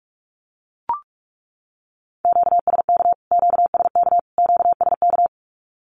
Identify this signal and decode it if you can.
Morse code, heard as a keyed tone, about 44 wpm, 700 Hz. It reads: QSK QSK QSK
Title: Q-Codes Repeated 3x 45wpm